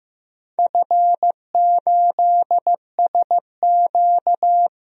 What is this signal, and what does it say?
Morse code, heard as a keyed tone, about 15 wpm, 700 Hz. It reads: F8SQ
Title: DXCC-100 Call Signs - 4 Characters - 15wpm